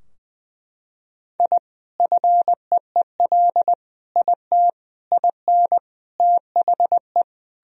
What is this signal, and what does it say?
Morse code, heard as a keyed tone, about 20 wpm, 700 Hz. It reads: I FEEL IT IN THE
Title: Sentences from Top 1000 Words - Review Part 2 - 20wpm